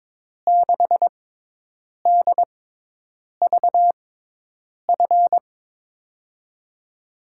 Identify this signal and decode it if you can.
Morse code, heard as a keyed tone, about 22 wpm, spaced as for 8 wpm, 700 Hz. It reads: 6DVF
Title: USA Call Signs - 6 Characters - 6x Character Spacing - 22wpm